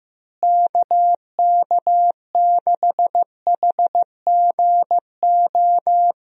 Morse code, heard as a keyed tone, about 15 wpm, 700 Hz. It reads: KK6HGO
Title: USA Call Signs - 6 Characters - Repeated 2x - 15wpm